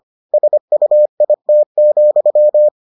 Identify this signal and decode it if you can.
Morse code, heard as a keyed tone, about 25 wpm, 600 Hz. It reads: SUIT,